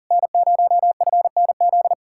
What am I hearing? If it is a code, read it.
Morse code, heard as a keyed tone, about 51 wpm, 700 Hz. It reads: N0FNZ